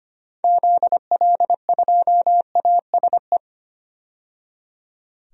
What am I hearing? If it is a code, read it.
Morse code, heard as a keyed tone, about 25 wpm, 700 Hz. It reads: ZL2ASE
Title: DXCC-100 Call Signs - 6 Characters - 25wpm